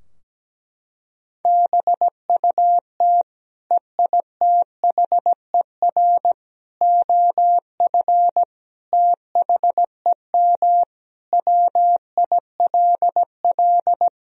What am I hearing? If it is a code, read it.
Morse code, heard as a keyed tone, about 17 wpm, 700 Hz. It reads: BUT EITHER OF THEM WILL
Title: Sentences from Top 1000 Words - Review Part 4 - 17wpm